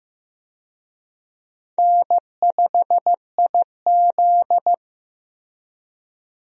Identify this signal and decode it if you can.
Morse code, heard as a keyed tone, about 15 wpm, 700 Hz. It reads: N5IZ
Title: USA Call Signs - 4 Characters - Repeated 2x - 15wpm